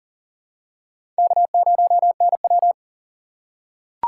Morse code, heard as a keyed tone, about 40 wpm, 700 Hz. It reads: K0NW
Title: K1USN SST QSO Practice - 40wpm